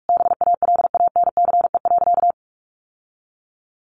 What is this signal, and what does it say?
Morse code, heard as a keyed tone, about 44 wpm, 700 Hz. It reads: BALANCE.
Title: Sentences from Top 1200 Words 45wpm